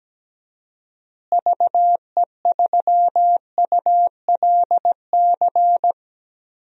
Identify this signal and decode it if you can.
Morse code, heard as a keyed tone, about 17 wpm, 700 Hz. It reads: VE3ULC
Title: DXCC-100 Call Signs - 6 Characters - Repeated 2x - 17wpm